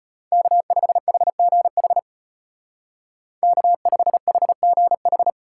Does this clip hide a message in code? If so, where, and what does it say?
Yes: Morse code from 0.3 to 5.3 s, heard as a keyed tone, about 33 wpm, 700 Hz. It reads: K5HGH K5HGH